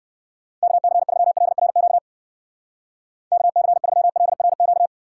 Morse code, heard as a keyed tone, about 59 wpm, 700 Hz. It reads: KC3LRQ KC3LRQ